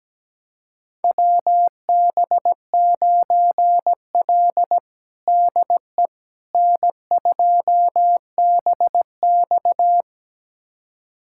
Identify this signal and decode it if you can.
Morse code, heard as a keyed tone, about 17 wpm, 700 Hz. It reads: WB9L DE N2BX
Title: USA Callsign DE Callsign - 4 Characters - 17wpm